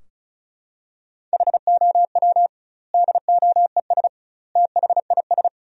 Morse code, heard as a keyed tone, about 34 wpm, 700 Hz. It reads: HOW DOES THIS
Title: 2x Word-Spacing - Sentences from Top 400 Words - 35wpm